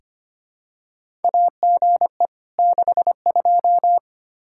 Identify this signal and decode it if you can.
Morse code, heard as a keyed tone, about 25 wpm, 700 Hz. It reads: AGE 62
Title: K1USN SST QSO Practice 25wpm